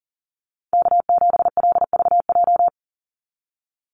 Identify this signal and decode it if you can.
Morse code, heard as a keyed tone, about 40 wpm, 700 Hz. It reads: K7LVJ
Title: Call Signs Repeated 3x - 40wpm